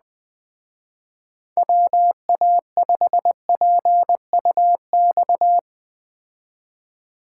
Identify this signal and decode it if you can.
Morse code, heard as a keyed tone, about 20 wpm, 700 Hz. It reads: WA5PUX
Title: Speed Racing - USA 2x3 Call Signs 17wpm